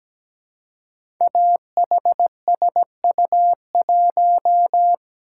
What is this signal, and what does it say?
Morse code, heard as a keyed tone, about 17 wpm, 700 Hz. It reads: AHSU1
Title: Code Groups - 5 Letter:Numbers - 17wpm